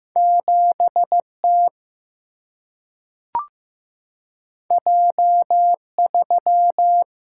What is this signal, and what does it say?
Morse code, heard as a keyed tone, about 15 wpm, 700 Hz. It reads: 7T J3
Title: Single-Letter Single-Number 15wpm